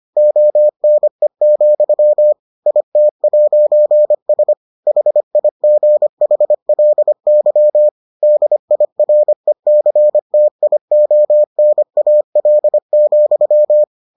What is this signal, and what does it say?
Morse code, heard as a keyed tone, about 25 wpm, 600 Hz. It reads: ONE, IT'S HIGHLY DIRECTIONAL,